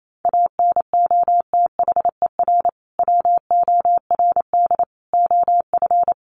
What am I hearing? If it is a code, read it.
Morse code, heard as a keyed tone, about 28 wpm, 700 Hz. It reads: ANOTHER WORD OF
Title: Sentences from Top 1100 Words 28wpm